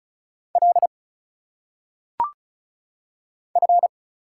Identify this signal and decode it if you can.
Morse code, heard as a keyed tone, about 34 wpm, 700 Hz. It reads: L F
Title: Single Letters - Phonetic Pronunciation 35wpm